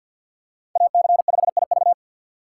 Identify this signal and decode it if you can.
Morse code, heard as a keyed tone, about 46 wpm, 700 Hz. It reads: AC5IV
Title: USA Call Signs - 5 Characters - 3x Character Spacing - 50wpm